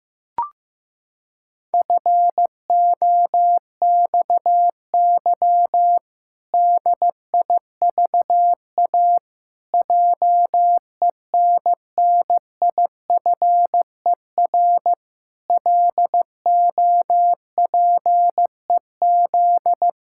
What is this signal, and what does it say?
Morse code, heard as a keyed tone, about 15 wpm, 700 Hz. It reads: FOXY DIVA JENNIFER LOPEZ
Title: Pangram Challenge v2 - 15wpm